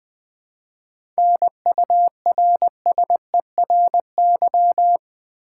Morse code, heard as a keyed tone, about 20 wpm, 700 Hz. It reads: NURSERY
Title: 7 Letter Words - First 3 Letters Given - 20wpm